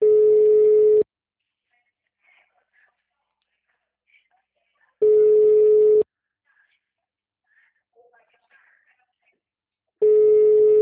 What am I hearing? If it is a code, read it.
Ringback tone (Europe, 425 Hz), with speech.